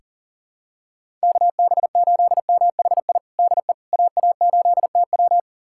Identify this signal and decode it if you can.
Morse code, heard as a keyed tone, about 37 wpm, 700 Hz. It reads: KB8MHI DE AA8TW
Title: USA Callsign DE Callsign 40wpm